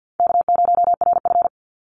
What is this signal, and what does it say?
Morse code, heard as a keyed tone, about 50 wpm, 700 Hz. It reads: K0RF
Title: Easy Call Signs Repeated 3x - 50wpm